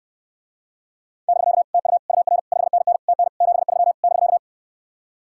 Morse code, heard as a keyed tone, about 69 wpm, 700 Hz. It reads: <BK> TU UR 5NN TN 73 <BK>